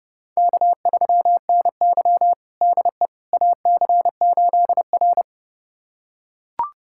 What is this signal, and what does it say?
Morse code, heard as a keyed tone, about 30 wpm, 700 Hz. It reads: K3NY DE AC8R
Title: USA Callsign DE Callsign - 4 Characters - 30wpm